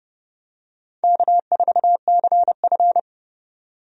Morse code, heard as a keyed tone, about 30 wpm, 700 Hz. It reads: K4CF
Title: USA Call Signs - 4 Characters - 30wpm